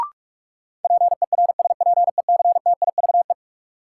Morse code, heard as a keyed tone, about 42 wpm, 700 Hz. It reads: PERSPECTIVE
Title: Sentences from Top 1200 Words - New Word Drill - 45wpm